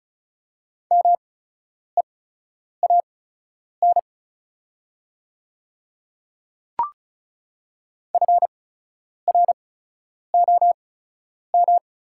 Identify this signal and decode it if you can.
Morse code, heard as a keyed tone, about 36 wpm, spaced as for 10 wpm, 690 Hz. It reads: MEAN FROM
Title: 8x Character Spacing - Top 200 Words - 35wpm